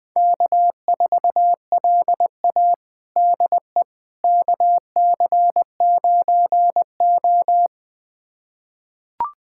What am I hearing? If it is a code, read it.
Morse code, heard as a keyed tone, about 21 wpm, 700 Hz. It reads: K4LA DE KC9O
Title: USA Callsign DE Callsign - 4 Characters - 20wpm